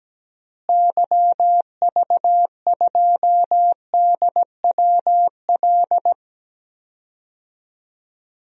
Morse code, heard as a keyed tone, about 17 wpm, 700 Hz. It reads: YV2DWL
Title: DXCC-100 Call Signs - 6 Characters - Repeated 3x - 17wpm